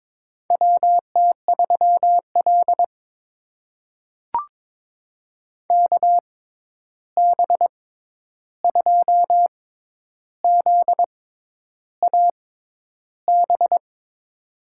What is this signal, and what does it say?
Morse code, heard as a keyed tone, about 22 wpm, spaced as for 10 wpm, 700 Hz. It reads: WT3L KB2ZAB
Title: USA Call Signs - 6x Character Spacing - 22wpm